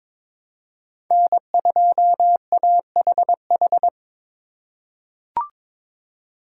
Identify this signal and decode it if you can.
Morse code, heard as a keyed tone, about 22 wpm, 700 Hz. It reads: N2AHH
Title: USA Call Signs - 5 Characters - 2x Character Spacing - 22wpm